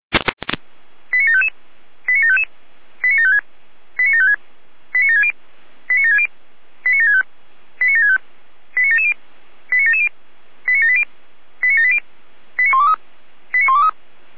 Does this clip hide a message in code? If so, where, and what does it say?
Yes: five-tone selective call (ZVEI) from 1.1 to 13.9 s, heard as quick runs of five tones, tones of about 80 ms. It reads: 89755 89755 89756 89756 89760 89760 89765 89765 89770 89770 89780 89780 89123 89123